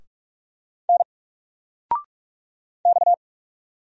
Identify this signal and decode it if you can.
Morse code, heard as a keyed tone, about 44 wpm, 700 Hz. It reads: N X
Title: Single Letter-Number 45wpm